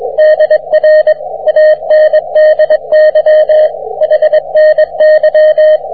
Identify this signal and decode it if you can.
Morse code, heard as a keyed tone, about 24 wpm, 620 Hz, with radio noise. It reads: DR ANDY HNY